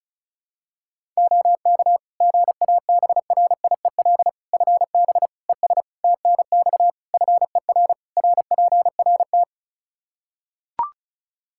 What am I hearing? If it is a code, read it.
Morse code, heard as a keyed tone, about 34 wpm, 700 Hz. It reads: OK GABRIEL FB ES TNX FER RPRT
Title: Practice QSO - Exchange 2 - Info - 35wpm